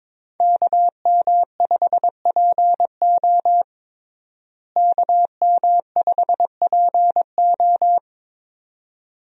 Morse code, heard as a keyed tone, about 22 wpm, 700 Hz. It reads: KM5PO KM5PO
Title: USA Call Signs - 5 Characters - Repeated 2x - 22wpm